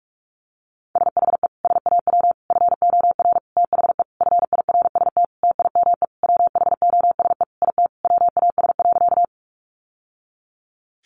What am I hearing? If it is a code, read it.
Morse code, heard as a keyed tone, about 44 wpm, 700 Hz. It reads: SHE SAW FOR THE FIRST TIME WHOSE IT WAS.
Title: Sentences from Top 1200 Words 45wpm